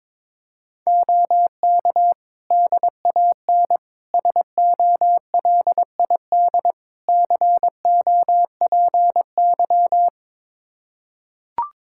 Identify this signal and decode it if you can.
Morse code, heard as a keyed tone, about 22 wpm, 700 Hz. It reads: OK DAN SOLID COPY
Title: Practice QSO - Exchange 3 - More Info - 22wpm